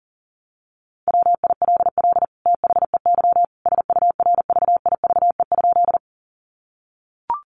Morse code, heard as a keyed tone, about 40 wpm, 700 Hz. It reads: WILL THEY SURVIVE?
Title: Sentences from Top 1300 Words 40wpm